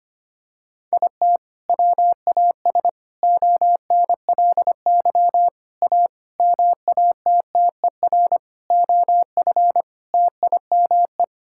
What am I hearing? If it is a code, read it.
Morse code, heard as a keyed tone, about 25 wpm, 700 Hz. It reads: IT WAS ONLY A MATTER OF TIME
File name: practice / 25wpm / Sentences with 9-Word Phrases - 25wpm